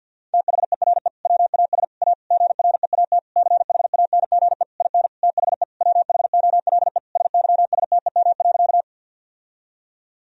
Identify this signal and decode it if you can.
Morse code, heard as a keyed tone, about 50 wpm, 700 Hz. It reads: THERE WAS A GREAT CHANGE IN THE WHOLE SYSTEM.
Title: Sentences from Top 1000 Words - Review Part 3 - 50wpm